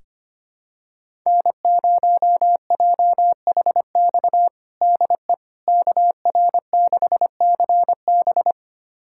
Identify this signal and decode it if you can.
Morse code, heard as a keyed tone, about 25 wpm, 700 Hz. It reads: N0JHX DE KR6CB
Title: USA Callsign DE Callsign - 5 Characters - 25wpm